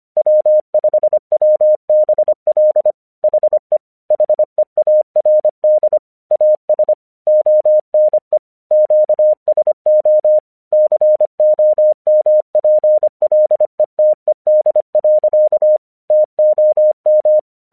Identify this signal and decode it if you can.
Morse code, heard as a keyed tone, about 26 wpm, 610 Hz. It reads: W5WBL HE HEARD AS ONE QSO COMPLETED. TOM
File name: 1 English 25wpm / AQC29